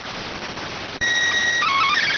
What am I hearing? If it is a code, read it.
SSTV (slow-scan television), PD 290.